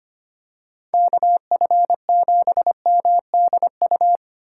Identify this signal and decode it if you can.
Morse code, heard as a keyed tone, about 25 wpm, 700 Hz. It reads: KF7MDU
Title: USA Call Signs - 6 Characters - 8x Character Spacing - 25wpm